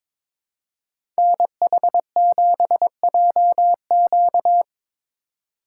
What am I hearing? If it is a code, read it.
Morse code, heard as a keyed tone, about 21 wpm, 700 Hz. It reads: NH7JQ
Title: USA Call Signs - 5 Characters - 3x Character Spacing - 22wpm